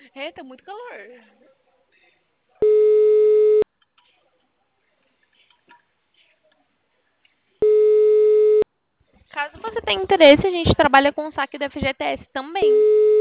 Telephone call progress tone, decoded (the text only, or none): ringback tone